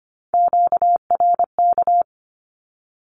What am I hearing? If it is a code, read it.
Morse code, heard as a keyed tone, about 25 wpm, 700 Hz. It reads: QRK
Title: Q-Codes with Spoken Meaning - 25wpm